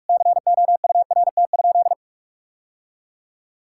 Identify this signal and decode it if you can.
Morse code, heard as a keyed tone, about 42 wpm, 700 Hz. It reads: COURT?
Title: Sentences from Top 1600 Words 45wpm